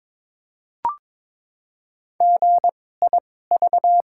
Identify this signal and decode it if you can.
Morse code, heard as a keyed tone, about 22 wpm, spaced as for 16 wpm, 700 Hz. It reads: GIV
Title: 2x Character Spacing - Top 200 Words - 22wpm